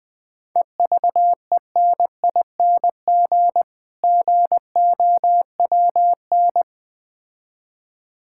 Morse code, heard as a keyed tone, about 20 wpm, 700 Hz. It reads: EVENING GOWN
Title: Difficult Words with Hints - 20wpm